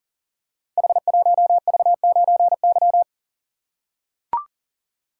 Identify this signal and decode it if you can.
Morse code, heard as a keyed tone, about 38 wpm, 700 Hz. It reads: H1V9Y
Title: Code Groups - 5 Letter:Numbers - 40wpm